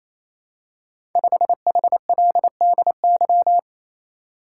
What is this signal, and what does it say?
Morse code, heard as a keyed tone, about 27 wpm, 700 Hz. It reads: 5HLDY
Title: Code Groups - 5 Letter:Numbers - 28wpm